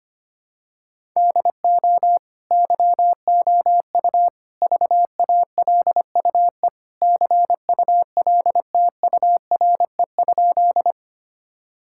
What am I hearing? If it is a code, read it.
Morse code, heard as a keyed tone, about 25 wpm, 700 Hz. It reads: DO YOU VALUE CULTURE?
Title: Sentences from Top 1000 Words - Review Part 6 - 25wpm